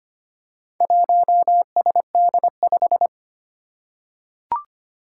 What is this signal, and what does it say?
Morse code, heard as a keyed tone, about 25 wpm, 700 Hz. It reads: 1SD5
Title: Four Challenging Letter-Number Combinations 25wpm